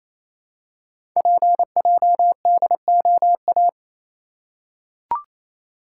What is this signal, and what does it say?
Morse code, heard as a keyed tone, about 27 wpm, 700 Hz. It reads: PJDOA
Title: Code Groups - 5 Letters - 28wpm